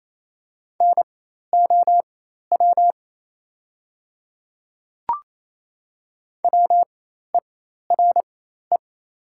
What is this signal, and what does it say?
Morse code, heard as a keyed tone, about 28 wpm, spaced as for 13 wpm, 700 Hz. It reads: NOW WERE
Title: 4x Character Spacing - Top 100 Words - 28wpm